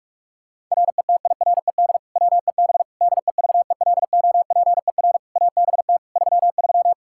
Morse code, heard as a keyed tone, about 42 wpm, 700 Hz. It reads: RETIRED WEB DEVELOPER ABT 33
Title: K1USN SST QSO Practice 45wpm